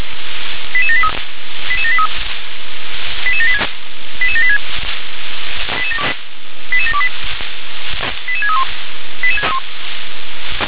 Five-tone selective call (ZVEI), heard as quick runs of five tones, tones of about 70 ms. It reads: 88783 88783 88786 88786 88728 88631
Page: /357/SDR/ZVEI/88645/2023.03.01/